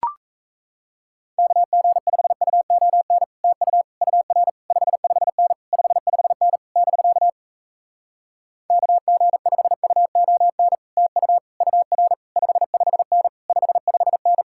Morse code, heard as a keyed tone, about 37 wpm, 700 Hz. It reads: KG5UON TU UR 55N 55N <BK> KG5UON TU UR 55N 55N